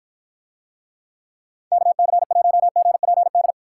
Morse code, heard as a keyed tone, about 53 wpm, 700 Hz. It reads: KC1GPD